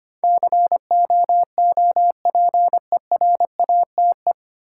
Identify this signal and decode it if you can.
Morse code, heard as a keyed tone, about 25 wpm, 700 Hz. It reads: COOPERATE
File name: Nine Letter Words - 25wpm